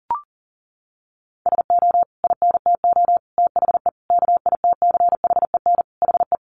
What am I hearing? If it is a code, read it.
Morse code, heard as a keyed tone, about 40 wpm, 700 Hz. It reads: SO INTO THE KITCHEN HE
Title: Sentences from Top 1400 Words 40wpm